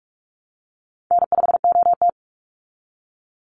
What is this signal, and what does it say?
Morse code, heard as a keyed tone, about 44 wpm, 700 Hz. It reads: N5OT
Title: Easy Call Signs Repeated 3x - 45wpm